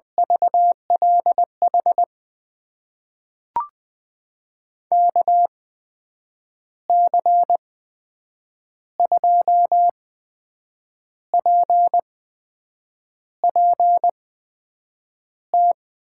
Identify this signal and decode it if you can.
Morse code, heard as a keyed tone, about 21 wpm, spaced as for 7 wpm, 700 Hz. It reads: VLH KC2PPT
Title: USA Call Signs - 6 Characters - 8x Character Spacing - 20wpm